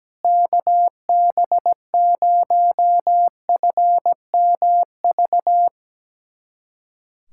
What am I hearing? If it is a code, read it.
Morse code, heard as a keyed tone, about 17 wpm, 700 Hz. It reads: KB0FMV
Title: USA Callsign DE Callsign 17wpm